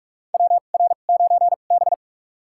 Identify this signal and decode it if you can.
Morse code, heard as a keyed tone, about 43 wpm, spaced as for 32 wpm, 700 Hz. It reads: WR9B
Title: USA Call Signs - 4 Characters - 2x Character Spacing - 45wpm